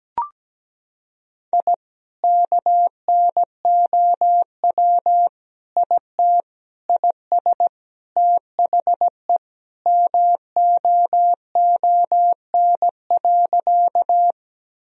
Morse code, heard as a keyed tone, about 17 wpm, 700 Hz. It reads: I KNOW IT IS THE MOON.
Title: Sentences from Top 1000 Words - Review Part 3 - 17wpm